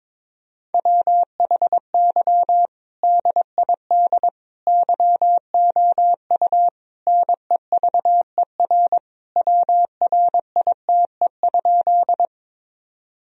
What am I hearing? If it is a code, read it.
Morse code, heard as a keyed tone, about 22 wpm, 700 Hz. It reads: WHY DID YOU NEVER WRITE?
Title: Sentences from Top 1000 Words - Review Part 1 - 22wpm